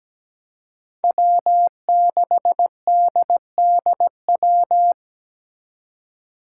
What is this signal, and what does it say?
Morse code, heard as a keyed tone, about 17 wpm, 700 Hz. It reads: W6DDW
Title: Five Challenging Letter-Number Combinations 17wpm